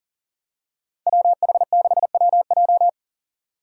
Morse code, heard as a keyed tone, about 40 wpm, 700 Hz. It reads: WH6WJ
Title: USA Call Signs - 5 Characters - 3x Character Spacing - 40wpm